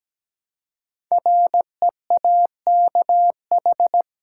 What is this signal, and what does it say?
Morse code, heard as a keyed tone, about 17 wpm, 700 Hz. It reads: REAKH
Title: Code Groups - 5 Letter:Numbers - 17wpm